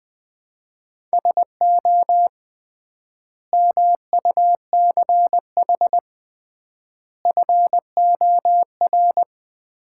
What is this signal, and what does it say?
Morse code, heard as a keyed tone, about 20 wpm, 700 Hz. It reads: SO MUCH FOR
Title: 3x Word-Spacing - Sets of 5 Words - 20wpm